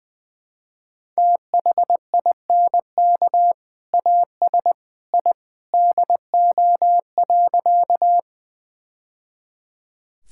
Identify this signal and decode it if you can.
Morse code, heard as a keyed tone, about 20 wpm, 700 Hz. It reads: THINK AS I DO.
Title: Sentences with 2-Word Phrases - 20wpm